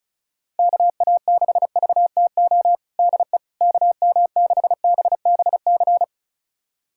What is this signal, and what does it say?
Morse code, heard as a keyed tone, about 34 wpm, 700 Hz. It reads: KA6VTO DE KM6BBC
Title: USA Callsign DE Callsign 35wpm